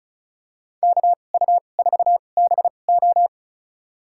Morse code, heard as a keyed tone, about 36 wpm, spaced as for 25 wpm, 700 Hz. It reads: KU4BO
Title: USA Call Signs - 5 Characters - 2x Character Spacing - 35wpm